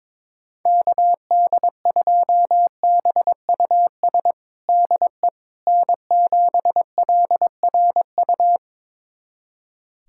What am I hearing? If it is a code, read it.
Morse code, heard as a keyed tone, about 22 wpm, 700 Hz. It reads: KD2BUS DE N7LRU
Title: USA Callsign DE Callsign 22wpm